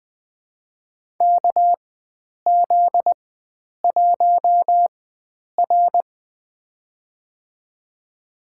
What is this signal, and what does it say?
Morse code, heard as a keyed tone, about 21 wpm, spaced as for 9 wpm, 700 Hz. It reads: KZ1R
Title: USA Call Signs - 4 Characters - 4x Character Spacing - 20wpm